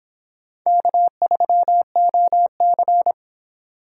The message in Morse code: K3OC

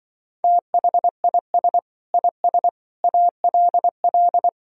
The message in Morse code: THIS IS ALL

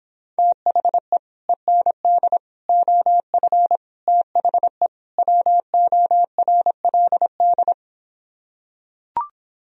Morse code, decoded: THE END OF THE WORLD

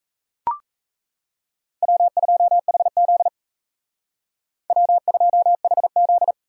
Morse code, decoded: W2HZ W2HZ

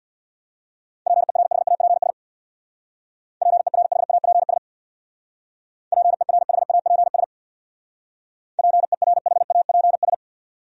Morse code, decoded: PERHAPS PERHAPS PERHAPS PERHAPS

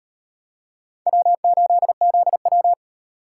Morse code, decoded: W8ZW

